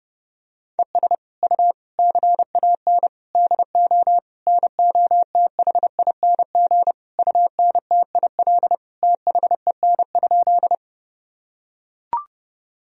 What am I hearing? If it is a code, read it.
Morse code, heard as a keyed tone, about 28 wpm, 700 Hz. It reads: ES U CAN DO NOTHING UNTIL THEN?
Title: Sentences from Top 500 Words and Abbreviations 30wpm